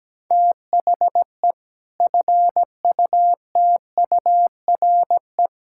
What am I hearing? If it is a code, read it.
Morse code, heard as a keyed tone, about 17 wpm, 700 Hz. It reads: THE FUTURE
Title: Sentences from Top 1000 Words - Review Part 6 - 17wpm